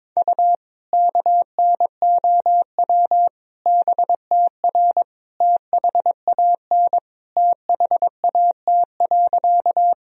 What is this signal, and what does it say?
Morse code, heard as a keyed tone, about 22 wpm, 700 Hz. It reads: U KNOW BTR THAN THAT.
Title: Sentences from Top 400 Words and Abbreviations 22wpm